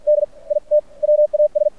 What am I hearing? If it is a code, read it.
Morse code, heard as a keyed tone, about 50 wpm, spaced as for 43 wpm, 600 Hz, with radio noise. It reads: D IT WAS
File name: archived / CRSnet